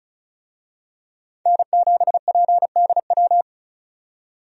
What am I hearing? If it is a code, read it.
Morse code, heard as a keyed tone, about 34 wpm, 700 Hz. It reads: N7PDW
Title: USA Call Signs - 5 Characters - 35wpm